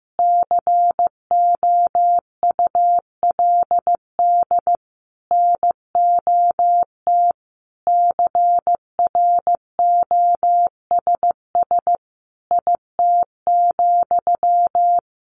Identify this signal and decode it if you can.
Morse code, heard as a keyed tone, about 15 wpm, 700 Hz. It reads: COULD NOT CROSS IT,
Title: Sentences from Top 1400 Words 15wpm